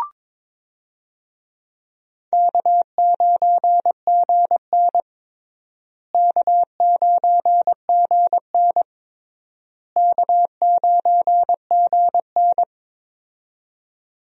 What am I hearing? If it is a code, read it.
Morse code, heard as a keyed tone, about 22 wpm, 700 Hz. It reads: K9GN K9GN K9GN
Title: USA Call Signs - 4 Characters - Repeated 3x - 22wpm